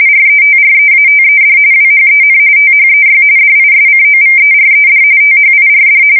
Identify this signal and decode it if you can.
RTTY, about 45.45 baud. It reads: RETREATED WHEN THEY DISCOVERED THIRD
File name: fsk170 / battleshipnj